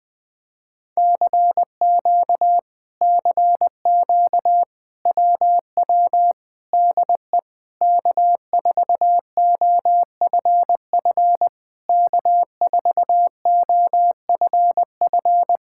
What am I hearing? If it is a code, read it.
Morse code, heard as a keyed tone, about 20 wpm, 700 Hz. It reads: CQ CQ WW DE K4OFF K4OFF
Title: Speed-Racing Common QSO Phrases Encore - 20wpm